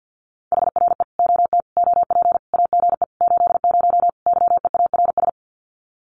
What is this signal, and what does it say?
Morse code, heard as a keyed tone, about 50 wpm, 700 Hz. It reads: HRE OT OP AGE 80 YEARS
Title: Common QSO Phrases 50wpm